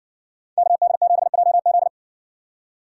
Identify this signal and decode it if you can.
Morse code, heard as a keyed tone, about 60 wpm, 700 Hz. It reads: KD7JZ